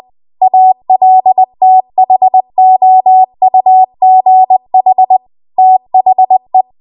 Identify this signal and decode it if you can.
Morse code, heard as a keyed tone, about 20 wpm, 750 Hz. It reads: ALTHOUGH THE
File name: Archive / 20 WPM / 20WPM